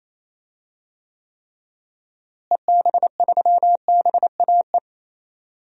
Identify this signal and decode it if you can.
Morse code, heard as a keyed tone, about 28 wpm, 700 Hz. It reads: EB3BAE
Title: DXCC-100 Call Signs - 6 Characters - Repeated 3x - 28wpm